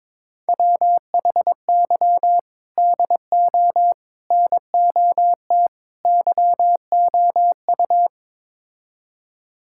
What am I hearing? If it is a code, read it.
Morse code, heard as a keyed tone, about 22 wpm, 700 Hz. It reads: WHY DO NOT YOU
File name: practice / 22wpm / Build A Phrase - 5-Word Phrases - 22wpm